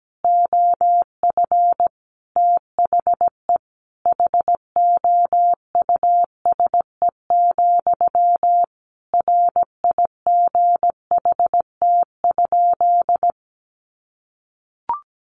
Morse code, heard as a keyed tone, about 17 wpm, 700 Hz. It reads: OF THE HOUSE, RIGHT?
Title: Sentences from Top 1200 Words 17wpm